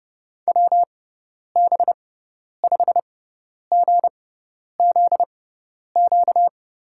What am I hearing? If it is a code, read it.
Morse code, heard as a keyed tone, about 31 wpm, spaced as for 10 wpm, 700 Hz. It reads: WB5GZQ
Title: USA Call Signs - 6 Characters - 6x Character Spacing - 30wpm